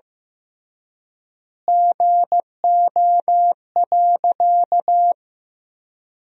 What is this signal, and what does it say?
Morse code, heard as a keyed tone, about 15 wpm, 700 Hz. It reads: GO.
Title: 3x Word-Spacing - Sentences from Top 100 Words - 15wpm